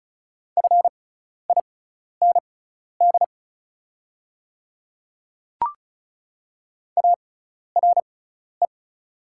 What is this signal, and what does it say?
Morse code, heard as a keyed tone, about 36 wpm, spaced as for 12 wpm, 700 Hz. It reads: FIND ARE
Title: 6x Character Spacing - Top 100 Words - 35wpm